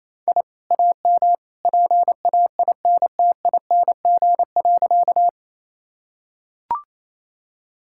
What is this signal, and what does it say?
Morse code, heard as a keyed tone, about 28 wpm, 700 Hz. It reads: I AM PAINTING.
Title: Sentences from Top 1000 Words - Review Part 6 - 28wpm